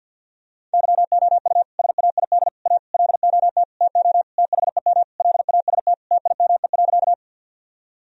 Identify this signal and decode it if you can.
Morse code, heard as a keyed tone, about 50 wpm, 700 Hz. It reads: YOU SAID A LOT TO THEM LAST TIME.
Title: Sentences from Top 1000 Words - Review Part 3 - 50wpm